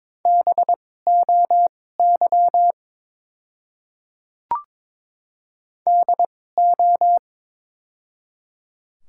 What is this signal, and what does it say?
Morse code, heard as a keyed tone, about 21 wpm, spaced as for 16 wpm, 700 Hz. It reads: BOY DO
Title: 2x Character Spacing - Top 200 Words - 22wpm